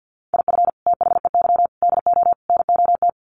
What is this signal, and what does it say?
Morse code, heard as a keyed tone, about 50 wpm, 700 Hz. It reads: IF THEY DO NOT
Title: Sets of 4 Words 50wpm